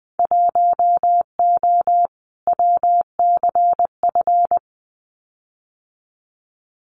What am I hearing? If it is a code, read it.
Morse code, heard as a keyed tone, about 20 wpm, 700 Hz. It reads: 1O WCF
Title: Winter Field Day Exchanges 20wpm